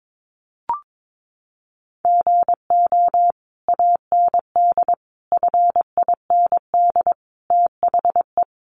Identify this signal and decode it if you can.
Morse code, heard as a keyed tone, about 22 wpm, 700 Hz. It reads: GO AND FIND THE
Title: Sentences from Top 1200 Words 22wpm